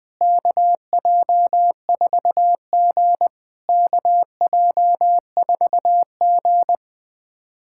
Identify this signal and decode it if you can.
Morse code, heard as a keyed tone, about 20 wpm, 690 Hz. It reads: KJ4G KJ4G
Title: SOTA QSO Practice 20wpm